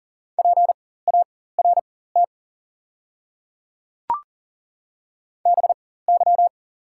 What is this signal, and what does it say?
Morse code, heard as a keyed tone, about 40 wpm, spaced as for 19 wpm, 690 Hz. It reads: PART BY
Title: 4x Character Spacing - Top 200 Words - 40wpm